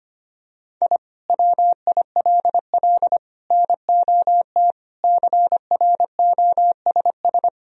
Morse code, heard as a keyed tone, about 25 wpm, 700 Hz. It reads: I WILL NOT CROSS
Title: Sentences from Top 1000 Words - Review Part 1 - 25wpm